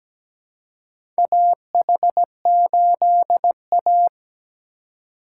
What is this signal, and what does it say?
Morse code, heard as a keyed tone, about 17 wpm, 700 Hz. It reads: AH8A
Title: USA Call Signs - 3x Character Spacing - 17wpm